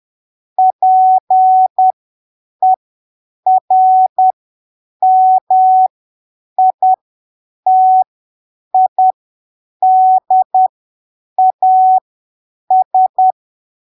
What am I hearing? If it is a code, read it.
Morse code, heard as a keyed tone, about 10 wpm, spaced as for 7 wpm, 750 Hz. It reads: PERMITIDAS
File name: arquivos / texto / texto 26 10ppm